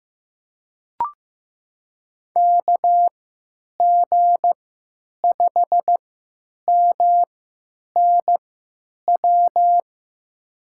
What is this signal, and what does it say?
Morse code, heard as a keyed tone, about 16 wpm, spaced as for 9 wpm, 700 Hz. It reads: KG5MNW
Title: USA Call Signs - 6 Characters - 3x Character Spacing - 15wpm